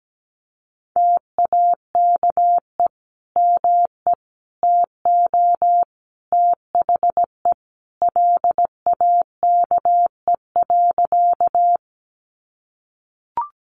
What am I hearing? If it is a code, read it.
Morse code, heard as a keyed tone, about 17 wpm, 700 Hz. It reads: TAKE ME TO THE LAKE.
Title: Sentences from Top 1100 Words 17wpm